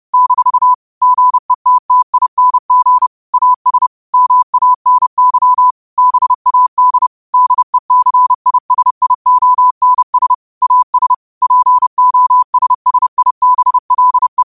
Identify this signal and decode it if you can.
Morse code, heard as a keyed tone, about 30 wpm, 1000 Hz. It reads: - GETTING AS MANY BAD DECISIONS AS POSSIBLE
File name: qotdfiles / 30wpm